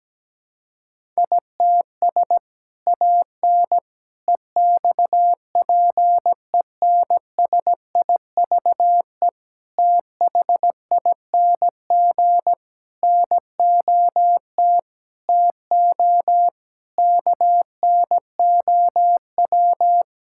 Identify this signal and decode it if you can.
Morse code, heard as a keyed tone, about 17 wpm, 700 Hz. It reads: ITS AN EXPENSIVE THING NOT TO KNOW
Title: Sentences from Top 1000 Words - Review Part 6 - 17wpm